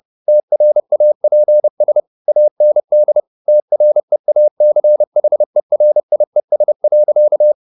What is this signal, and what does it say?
Morse code, heard as a keyed tone, about 30 wpm, 600 Hz. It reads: TRAPS AND TREACHERIES.
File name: the old man and the sea / the old man and the sea 1 30wpm